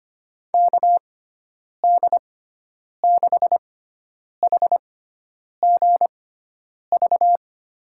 Morse code, heard as a keyed tone, about 25 wpm, spaced as for 9 wpm, 700 Hz. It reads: KD6HGV
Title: USA Call Signs - 6x Character Spacing - 25wpm